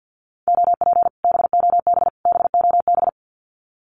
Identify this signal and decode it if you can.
Morse code, heard as a keyed tone, about 50 wpm, 700 Hz. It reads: OP BOB BOB
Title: Common QSO Phrases Encore 50wpm